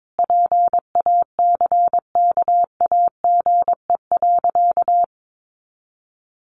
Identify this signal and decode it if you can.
Morse code, heard as a keyed tone, about 22 wpm, 700 Hz. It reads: PACKAGE.
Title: Easy Phrases 22wpm